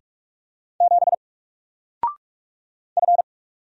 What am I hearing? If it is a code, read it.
Morse code, heard as a keyed tone, about 44 wpm, 700 Hz. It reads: 7 F
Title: Single Character (Letter:Number:Punctuation:Prosign) 45wpm